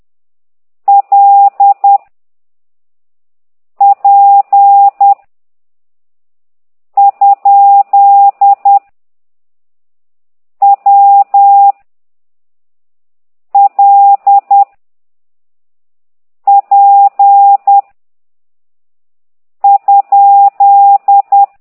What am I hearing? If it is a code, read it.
Morse code, heard as a keyed tone, about 10 wpm, 800 Hz. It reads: L P ? W L P ?